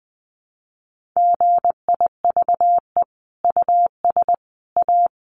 Morse code, heard as a keyed tone, about 20 wpm, 700 Hz. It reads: GIVE US A
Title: Sentences from Top 1500 Words 20wpm